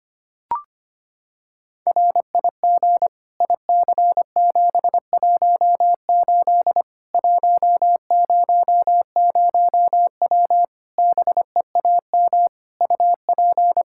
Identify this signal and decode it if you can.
Morse code, heard as a keyed tone, about 25 wpm, 700 Hz. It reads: RIG IC718 100W BEAM UP
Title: Practice QSO - Exchange 2 - Info - 25wpm